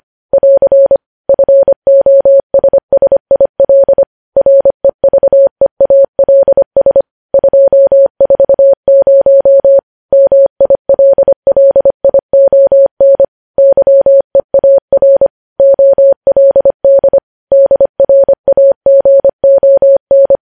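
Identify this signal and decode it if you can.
Morse code, heard as a keyed tone, about 25 wpm, 550 Hz. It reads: <AR> FOSSIL REVEALS 240 MILLION YEAR OLD DRAGON
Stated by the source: Morse code practice of news headlines at 25 wpm, opening with vvv.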